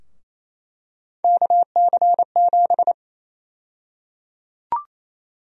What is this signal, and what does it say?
Morse code, heard as a keyed tone, about 28 wpm, 700 Hz. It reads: KC7
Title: USA Call Sign Prefixes - 28wpm